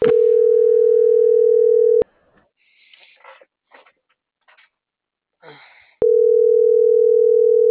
Ringback tone (North America), 440 + 480 Hz.
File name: CENTRAL MAERSK / out luis.herrera - .30368